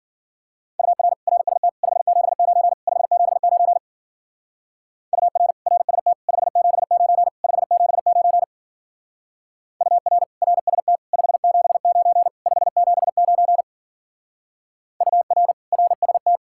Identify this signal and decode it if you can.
Morse code, heard as a keyed tone, about 50 wpm, 700 Hz. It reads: UR RST 579 579 UR RST 579 579 UR RST 579 579 UR RST
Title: Speed-Racing Common QSO Phrases - 40wpm